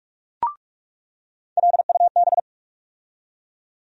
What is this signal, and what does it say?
Morse code, heard as a keyed tone, about 46 wpm, 700 Hz. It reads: LUB
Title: USA Call Sign Postfixes - 3 Letters 45wpm